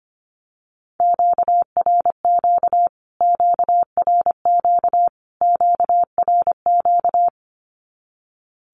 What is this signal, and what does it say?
Morse code, heard as a keyed tone, about 25 wpm, 700 Hz. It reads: QRQ QRQ QRQ
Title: Q-Codes Repeated 3x 25wpm